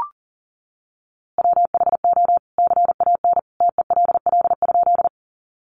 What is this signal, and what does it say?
Morse code, heard as a keyed tone, about 40 wpm, 700 Hz. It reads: WHO CAN TELL?
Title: A Sound Of Thunder 40wpm